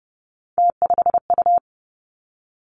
Morse code, heard as a keyed tone, about 32 wpm, 700 Hz. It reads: T5U
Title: Three Challenging Letter-Number Combinations 30wpm